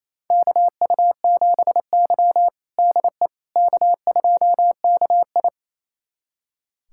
Morse code, heard as a keyed tone, about 27 wpm, 690 Hz. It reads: KU7Y DE K2KI
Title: USA Callsign DE Callsign - 4 Characters - 28wpm